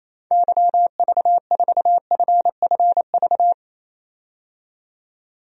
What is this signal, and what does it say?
Morse code, heard as a keyed tone, about 28 wpm, 700 Hz. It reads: YV4FFV
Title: DXCC-100 Call Signs - 6 Characters - 28wpm